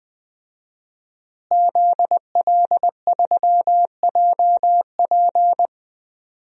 Morse code, heard as a keyed tone, about 20 wpm, 700 Hz. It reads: ZL3JP
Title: DXCC-100 Call Signs - 5 Characters - Repeated 2x - 20wpm